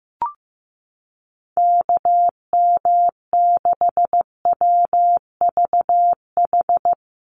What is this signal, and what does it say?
Morse code, heard as a keyed tone, about 15 wpm, 700 Hz. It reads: KM6WVH
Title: USA Call Signs 15wpm